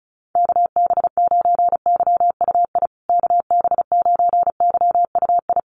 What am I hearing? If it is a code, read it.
Morse code, heard as a keyed tone, about 34 wpm, 700 Hz. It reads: KB9YUI KB9YUI
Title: USA Call Signs Repeated 3x 35wpm